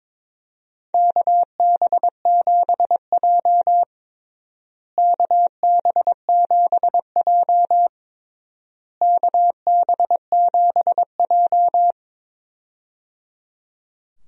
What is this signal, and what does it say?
Morse code, heard as a keyed tone, about 22 wpm, 700 Hz. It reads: KB7J KB7J KB7J
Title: USA Call Signs - 4 Characters - Repeated 3x - 22wpm